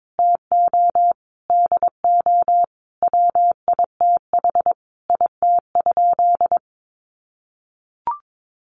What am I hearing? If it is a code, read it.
Morse code, heard as a keyed tone, about 22 wpm, 700 Hz. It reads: TO DO WITH IT?
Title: Sentences from Top 100 Words 22wpm